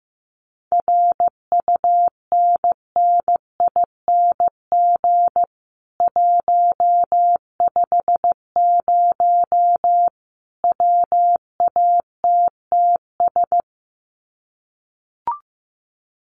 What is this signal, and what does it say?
Morse code, heard as a keyed tone, about 15 wpm, 700 Hz. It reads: RUNNING 150 WATTS
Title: Common QSO Phrases 15wpm